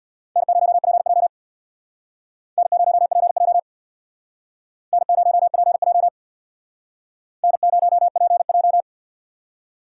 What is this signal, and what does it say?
Morse code, heard as a keyed tone, about 61 wpm, 690 Hz. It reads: N0PJ N0PJ N0PJ N0PJ
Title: Speed Racing - USA 1x2 Call Signs 50wpm